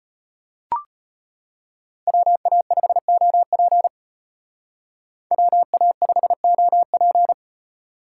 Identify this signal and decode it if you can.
Morse code, heard as a keyed tone, about 31 wpm, 700 Hz. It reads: WA5OP WA5OP
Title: Speed Racing - USA 2x2 Call Signs 25wpm